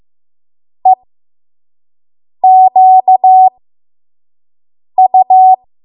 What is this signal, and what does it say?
Morse code, heard as a keyed tone, about 15 wpm, spaced as for 5 wpm, 750 Hz. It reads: EQU
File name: Morse / 05WPM